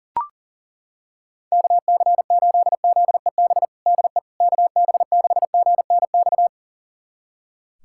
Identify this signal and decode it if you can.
Morse code, heard as a keyed tone, about 37 wpm, 700 Hz. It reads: KC8ZEB DE KB6GNX
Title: USA Callsign DE Callsign 40wpm